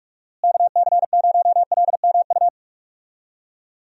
Morse code, heard as a keyed tone, about 46 wpm, 700 Hz. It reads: KC0LMU
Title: USA Call Signs - 6 Characters - Repeated 3x - 45wpm